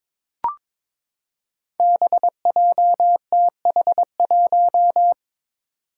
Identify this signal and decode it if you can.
Morse code, heard as a keyed tone, about 22 wpm, 700 Hz. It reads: BJTH1
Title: Five Challenging Letter-Number Combinations 22wpm